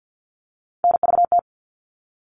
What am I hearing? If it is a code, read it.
Morse code, heard as a keyed tone, about 50 wpm, 700 Hz. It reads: NVT
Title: All Cut Numbers - 3-Digit 50wpm